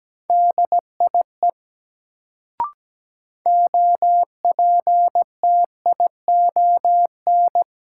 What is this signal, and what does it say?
Morse code, heard as a keyed tone, about 17 wpm, 700 Hz. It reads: DIE OPTION
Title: Sentences from Top 1300 Words - New Word Drill - 17wpm